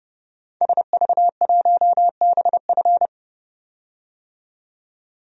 Morse code, heard as a keyed tone, about 29 wpm, 700 Hz. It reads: SV1BF
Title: DXCC-100 Call Signs - 5 Characters - 30wpm